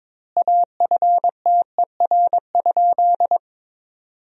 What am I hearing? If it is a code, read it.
Morse code, heard as a keyed tone, about 22 wpm, 700 Hz. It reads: AFTER?
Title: 4x Word-Spacing - Sentences from Top 200 Words - 22wpm